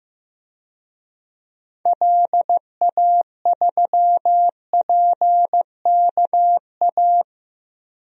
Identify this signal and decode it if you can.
Morse code, heard as a keyed tone, about 15 wpm, 700 Hz. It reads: LA3PKA
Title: DXCC-100 Call Signs - 6 Characters - Repeated 2x - 15wpm